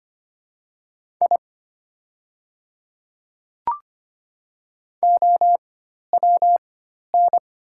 Morse code, heard as a keyed tone, about 25 wpm, spaced as for 12 wpm, 690 Hz. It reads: I OWN
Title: 4x Character Spacing - Top 300 Words - 25wpm